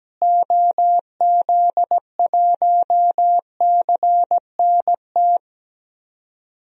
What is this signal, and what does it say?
Morse code, heard as a keyed tone, about 17 wpm, 700 Hz. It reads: OZ1CNT
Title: DXCC-100 Call Signs - 6 Characters - Repeated 2x - 17wpm